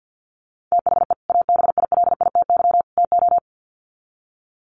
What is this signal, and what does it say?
Morse code, heard as a keyed tone, about 49 wpm, 700 Hz. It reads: THE ABILITY TO
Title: Sets of 3 Words 50wpm